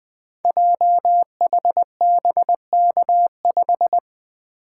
Morse code, heard as a keyed tone, about 20 wpm, 700 Hz. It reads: JHBK5
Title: Code Groups - 5 Letter:Numbers - 20wpm